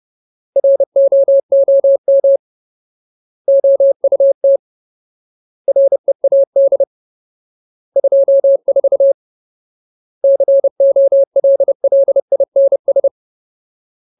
Morse code, heard as a keyed tone, about 30 wpm, 550 Hz. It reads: ROOM OUT READ 24 COLLINS
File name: Ultimate Word List / 30WPM Ultimate